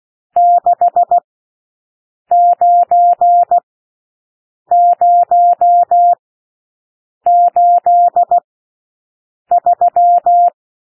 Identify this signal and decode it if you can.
Morse code, heard as a keyed tone, about 16 wpm, spaced as for 6 wpm, 700 Hz. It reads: 69083
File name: SOUND NUMBERS 16 wpm